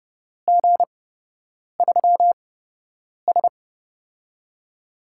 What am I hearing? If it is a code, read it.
Morse code, heard as a keyed tone, about 30 wpm, spaced as for 8 wpm, 700 Hz. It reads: G3S
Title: USA Call Signs - 4 Characters - 8x Character Spacing - 30wpm